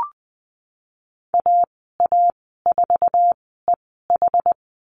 Morse code, heard as a keyed tone, about 20 wpm, spaced as for 15 wpm, 700 Hz. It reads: AA4EH
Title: 069 Lesson 12g - Letter d - Call Signs 20wpm